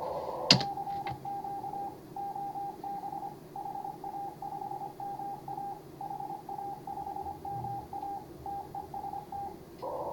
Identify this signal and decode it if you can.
Morse code, heard as a keyed tone, about 22 wpm, 800 Hz, with noise. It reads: CQ OB HA5MA HA5MA TEST
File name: CW / HA5MA RHOB HANG CW 2017